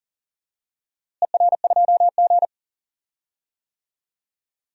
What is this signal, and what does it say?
Morse code, heard as a keyed tone, about 40 wpm, 690 Hz. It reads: ER2G
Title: DXCC-100 Call Signs - 4 Characters - Repeated 2x - 40wpm